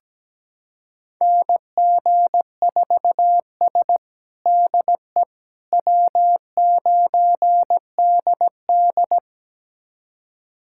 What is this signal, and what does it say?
Morse code, heard as a keyed tone, about 17 wpm, 700 Hz. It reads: NG4S DE W9DD
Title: USA Callsign DE Callsign - 4 Characters - 17wpm